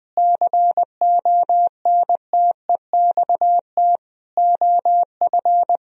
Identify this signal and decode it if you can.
Morse code, heard as a keyed tone, about 20 wpm, 700 Hz. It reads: CONTEXT OF
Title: Speed-Racing - Sets of 4 Words 20wpm